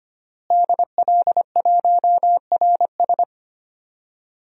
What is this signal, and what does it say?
Morse code, heard as a keyed tone, about 25 wpm, 700 Hz. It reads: DL1RS
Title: DXCC-100 Call Signs - 5 Characters - Repeated 3x - 25wpm